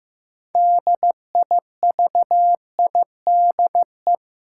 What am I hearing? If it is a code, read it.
Morse code, heard as a keyed tone, about 15 wpm, 700 Hz. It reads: DIVIDE
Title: Sentences from Top 1600 Words - New Word Drill Alt - 15wpm